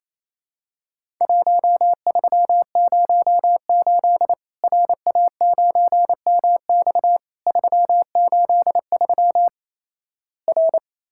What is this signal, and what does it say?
Morse code, heard as a keyed tone, about 28 wpm, 700 Hz. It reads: 1308 RA9MX 383 R
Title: Worked All Europe DX Contest - QTC Exchange - 28wpm